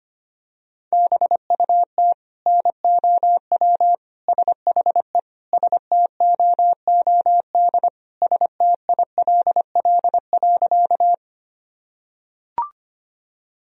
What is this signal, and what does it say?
Morse code, heard as a keyed tone, about 25 wpm, 700 Hz. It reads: BUT NOW SHE STOOD STILL.
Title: Sentences from Top 1000 Words - Review Part 2 - 25wpm